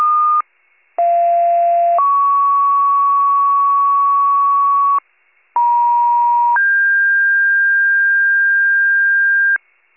Two-tone page, heard as a one-second tone, then a three-second tone, about 690 then 1120 Hz, about 950 then 1600 Hz, with noise. It is heard from 1.0 to 9.6 s.